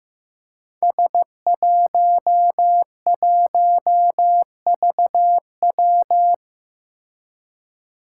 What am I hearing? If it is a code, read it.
Morse code, heard as a keyed tone, about 15 wpm, 700 Hz. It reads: S11VW
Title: Five Challenging Letter-Number Combinations 15wpm